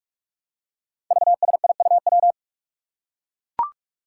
Morse code, heard as a keyed tone, about 44 wpm, 700 Hz. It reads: VSIUW
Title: Five Challenging Letter-Number Combinations 45wpm